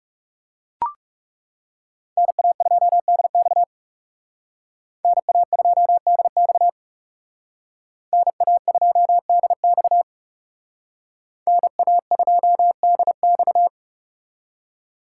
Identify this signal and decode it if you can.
Morse code, heard as a keyed tone, about 38 wpm, 700 Hz. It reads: NA2DX NA2DX NA2DX NA2DX